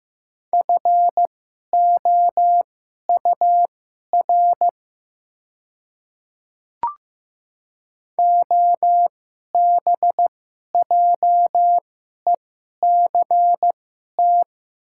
Morse code, heard as a keyed tone, about 15 wpm, spaced as for 11 wpm, 700 Hz. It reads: FOUR OBJECT
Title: 2x Character Spacing - Top 500 Words - 15wpm